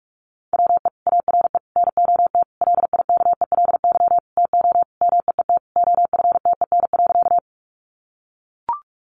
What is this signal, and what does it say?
Morse code, heard as a keyed tone, about 44 wpm, 700 Hz. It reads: WE ARE NOT LIKELY TO MEET OFTEN.
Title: Sentences from Top 1100 Words 45wpm